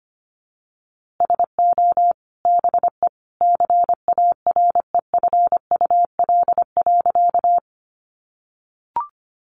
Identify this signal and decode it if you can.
Morse code, heard as a keyed tone, about 25 wpm, 700 Hz. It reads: SO BE CAREFUL.
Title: A Sound Of Thunder 25wpm